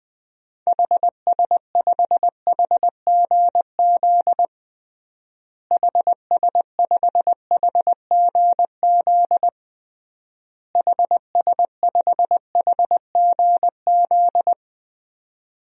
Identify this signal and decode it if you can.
Morse code, heard as a keyed tone, about 20 wpm, 700 Hz. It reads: HS5HGZ HS5HGZ HS5HGZ
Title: DXCC-100 Call Signs - 6 Characters - Repeated 3x - 20wpm